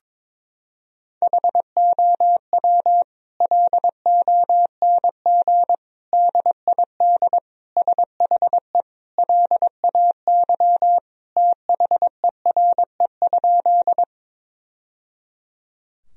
Morse code, heard as a keyed tone, about 21 wpm, 700 Hz. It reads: HOW LONG DID SHE LAY THERE?
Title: Sentences from Top 1000 Words - Review Part 2 - 22wpm